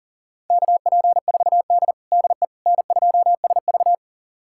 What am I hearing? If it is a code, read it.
Morse code, heard as a keyed tone, about 37 wpm, 700 Hz. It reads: KP4D DE N2SV
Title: USA Callsign DE Callsign - 4 Characters - 40wpm